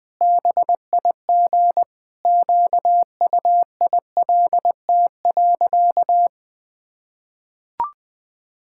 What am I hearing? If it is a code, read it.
Morse code, heard as a keyed tone, about 21 wpm, 700 Hz. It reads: BIG QUILT.
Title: Pangram Challenge v2 - 20wpm